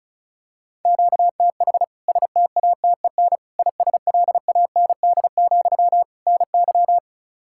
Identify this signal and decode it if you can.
Morse code, heard as a keyed tone, about 36 wpm, 700 Hz. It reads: QTH STATEN ISLAND, NY
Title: K1USN SST QSO Practice 35wpm